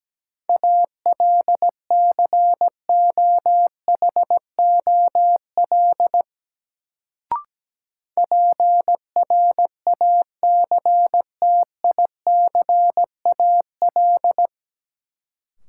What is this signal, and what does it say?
Morse code, heard as a keyed tone, about 17 wpm, 700 Hz. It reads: ALCOHOL PRACTICAL
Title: Sentences from Top 1400 Words - New Word Drill Alt - 17wpm